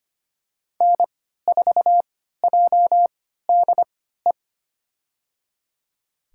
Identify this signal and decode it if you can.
Morse code, heard as a keyed tone, about 25 wpm, spaced as for 14 wpm, 700 Hz. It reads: N4JDE
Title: USA Call Signs - 5 Characters - 3x Character Spacing - 25wpm